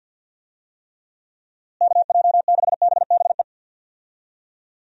Morse code, heard as a keyed tone, about 52 wpm, 700 Hz. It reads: KJ6BBE
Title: USA Call Signs - 6 Characters - Repeated 3x - 50wpm